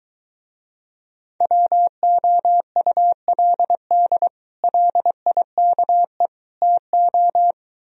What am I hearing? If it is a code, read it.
Morse code, heard as a keyed tone, about 23 wpm, 700 Hz. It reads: WOULD LIKE TO